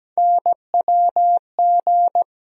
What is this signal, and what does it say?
Morse code, heard as a keyed tone, about 17 wpm, 700 Hz. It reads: NWG
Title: USA Call Sign Postfixes - 3 Letters 17wpm